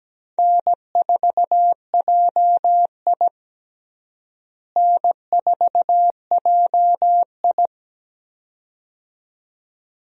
Morse code, heard as a keyed tone, about 17 wpm, 700 Hz. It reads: N4JI N4JI
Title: USA Call Signs - 4 Characters - Repeated 2x - 17wpm